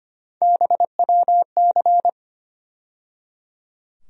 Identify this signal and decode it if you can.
Morse code, heard as a keyed tone, about 25 wpm, 700 Hz. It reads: BWC
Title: USA Call Sign Postfixes - 3 Letters 25wpm